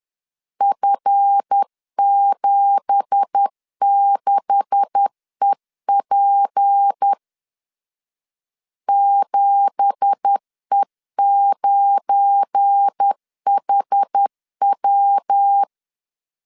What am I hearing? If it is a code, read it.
Morse code, heard as a keyed tone, about 10 wpm, 800 Hz. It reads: F76EP 7E9HW
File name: LEZIONI / 9 2867 10wpm 2733